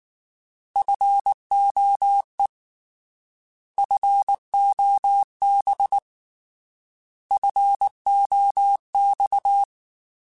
Morse code, heard as a keyed tone, about 19 wpm, 800 Hz. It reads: FOE FOB FOX